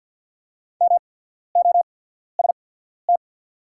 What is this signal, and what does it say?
Morse code, heard as a keyed tone, about 50 wpm, spaced as for 14 wpm, 690 Hz. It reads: MOST
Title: 8x Character Spacing - Top 100 Words - 50wpm